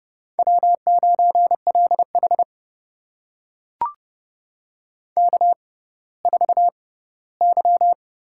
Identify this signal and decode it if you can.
Morse code, heard as a keyed tone, about 30 wpm, spaced as for 17 wpm, 700 Hz. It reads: W9LH K4Y
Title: USA Call Signs - 4 Characters - 6x Character Spacing - 30wpm